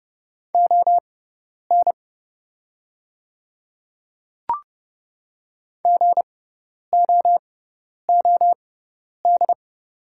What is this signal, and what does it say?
Morse code, heard as a keyed tone, about 30 wpm, spaced as for 10 wpm, 700 Hz. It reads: ON GOOD
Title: 6x Character Spacing - Top 200 Words - 30wpm